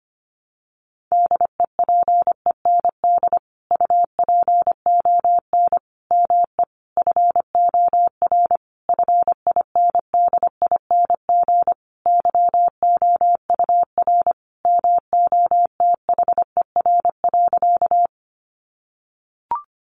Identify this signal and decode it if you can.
Morse code, heard as a keyed tone, about 25 wpm, 700 Hz. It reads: DEPEND UPON ME FOR FINDING YOUR MOTHER.
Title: Sentences from Top 1300 Words 25wpm